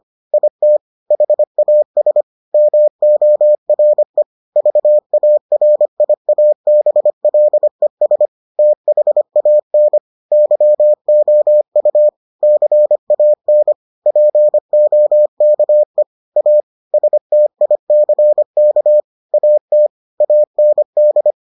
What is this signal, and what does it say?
Morse code, heard as a keyed tone, about 25 wpm, 600 Hz. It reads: IT HAS MORE VARIABLES THAN YOU CAN POKE A STICK AT AND